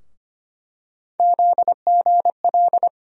Morse code, heard as a keyed tone, about 25 wpm, 700 Hz. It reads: ZGL
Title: USA Call Sign Postfixes - 3 Letters 25wpm